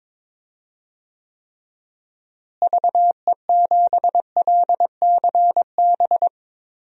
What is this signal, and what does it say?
Morse code, heard as a keyed tone, about 22 wpm, 700 Hz. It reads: VE7LCB
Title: DXCC-100 Call Signs - 6 Characters - Repeated 3x - 22wpm